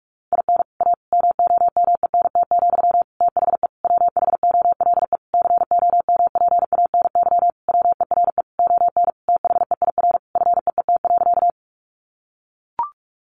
Morse code, heard as a keyed tone, about 44 wpm, 700 Hz. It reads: IN A MOMENT, THE WHOLE COMPANY WERE ON THEIR FEET.
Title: Sentences from Top 1000 Words 45wpm